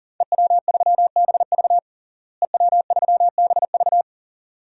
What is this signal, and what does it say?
Morse code, heard as a keyed tone, about 40 wpm, 700 Hz. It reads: EW3BV EW3BV
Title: DXCC-100 Call Signs - 5 Characters - Repeated 2x - 40wpm